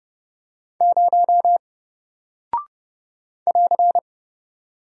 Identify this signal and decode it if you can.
Morse code, heard as a keyed tone, about 30 wpm, 700 Hz. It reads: 0 <AR>
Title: Single Character (Letter:Number:Punctuation:Prosign) Alt 30wpm